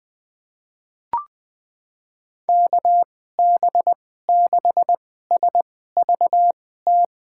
Morse code, heard as a keyed tone, about 21 wpm, spaced as for 15 wpm, 700 Hz. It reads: KB6SVT
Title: USA Call Signs - 6 Characters - 2x Character Spacing - 20wpm